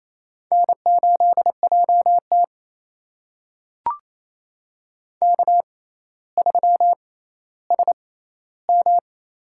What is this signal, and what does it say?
Morse code, heard as a keyed tone, about 27 wpm, spaced as for 14 wpm, 700 Hz. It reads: N8JT K3SM
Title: USA Call Signs - 4 Characters - 6x Character Spacing - 28wpm